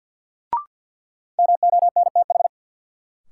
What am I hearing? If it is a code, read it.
Morse code, heard as a keyed tone, about 50 wpm, 700 Hz. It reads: MONTH
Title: Sentences from Top 800 Words - New Word Drill - 50wpm